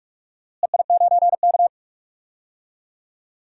Morse code, heard as a keyed tone, about 43 wpm, 700 Hz. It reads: EI9K
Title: DXCC-100 Call Signs - 4 Characters - Repeated 3x - 45wpm